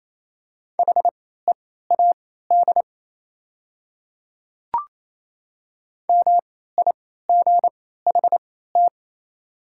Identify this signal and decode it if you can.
Morse code, heard as a keyed tone, about 28 wpm, spaced as for 16 wpm, 690 Hz. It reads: HEAD MIGHT
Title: 3x Character Spacing - Top 300 Words - 28wpm